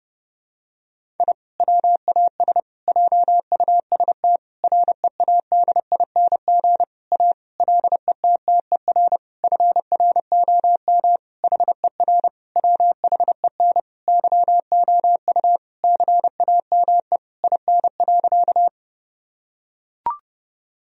Morse code, heard as a keyed tone, about 32 wpm, 700 Hz. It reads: I WAS JUST READING A LETTER FROM HER WHEN YOU CAME IN.
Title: Sentences from Top 1000 Words - Review Part 3 - 30wpm